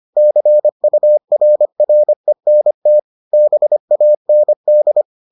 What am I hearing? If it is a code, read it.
Morse code, heard as a keyed tone, about 25 wpm, 600 Hz. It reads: CURRENT BAND